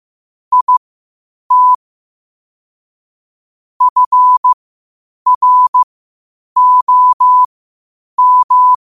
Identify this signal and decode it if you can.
Morse code, heard as a keyed tone, about 15 wpm, spaced as for 8 wpm, 1000 Hz. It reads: IT FROM
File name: qotdfiles / 8wpm